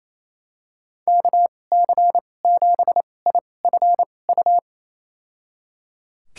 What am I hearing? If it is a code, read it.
Morse code, heard as a keyed tone, about 28 wpm, spaced as for 20 wpm, 700 Hz. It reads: KC7IFU
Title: USA Call Signs - 6 Characters - 2x Character Spacing - 28wpm